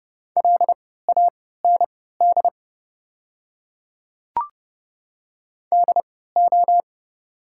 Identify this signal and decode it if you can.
Morse code, heard as a keyed tone, about 32 wpm, spaced as for 17 wpm, 690 Hz. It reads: LAND DO
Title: 3x Character Spacing - Top 200 Words - 30wpm